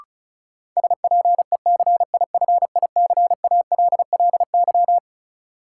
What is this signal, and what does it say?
Morse code, heard as a keyed tone, about 34 wpm, 700 Hz. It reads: SPECIFICALLY
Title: Sentences from Top 1100 Words - New Word Drill - 35wpm